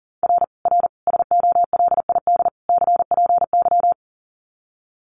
Morse code, heard as a keyed tone, about 37 wpm, 700 Hz. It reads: R R SOLID CPY
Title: Common QSO Phrases 40wpm